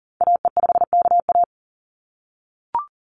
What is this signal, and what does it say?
Morse code, heard as a keyed tone, about 40 wpm, 700 Hz. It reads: AE5KA
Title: NA SOTA Activator Callsigns 40wpm